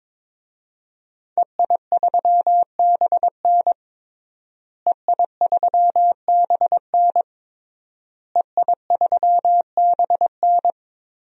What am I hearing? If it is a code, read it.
Morse code, heard as a keyed tone, about 21 wpm, 700 Hz. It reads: EI3BN EI3BN EI3BN
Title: DXCC-100 Call Signs - 5 Characters - Repeated 3x - 22wpm